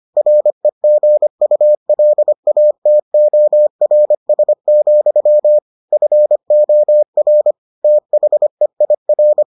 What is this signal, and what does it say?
Morse code, heard as a keyed tone, about 25 wpm, 600 Hz. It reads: REGULATORS, FOR THEIR